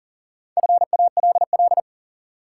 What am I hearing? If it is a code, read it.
Morse code, heard as a keyed tone, about 40 wpm, 700 Hz. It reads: FALL
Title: Sentences from Top 500 Words - New Word Drill - 40wpm